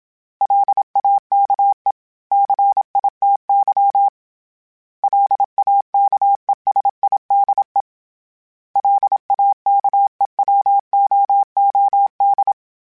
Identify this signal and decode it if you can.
Morse code, heard as a keyed tone, about 27 wpm, 800 Hz. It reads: LAKE CITY LAKESIDE LAKEWOOD
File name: fromSMs / FromSMs mp3s / Cities@25